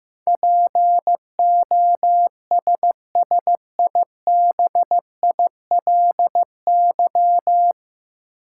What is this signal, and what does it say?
Morse code, heard as a keyed tone, about 15 wpm, 700 Hz. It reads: POSSIBILY
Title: Sentences from Top 1300 Words - New Word Drill Alt - 15wpm